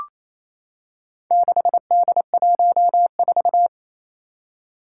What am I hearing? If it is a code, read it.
Morse code, heard as a keyed tone, about 28 wpm, 700 Hz. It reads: 6D14
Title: Four Challenging Letter-Number Combinations 28wpm